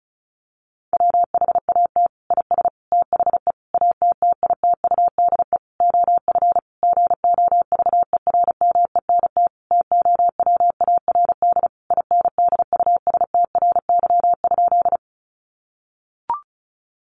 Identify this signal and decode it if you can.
Morse code, heard as a keyed tone, about 34 wpm, 700 Hz. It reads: WHAT IS THE ATTITUDE OF GOVERMENT TOWARD INDUSTRY?
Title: Sentences from Top 1300 Words 35wpm